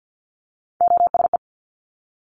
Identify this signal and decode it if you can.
Morse code, heard as a keyed tone, about 50 wpm, 700 Hz. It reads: OSE
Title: Top Three Letter Combinations in Words 50wpm